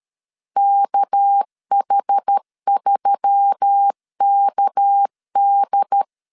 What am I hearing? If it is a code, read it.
Morse code, heard as a keyed tone, about 13 wpm, 800 Hz. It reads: KH3KD
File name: LEZIONI / 6 PKX0 / 6 PK 12wpm 2006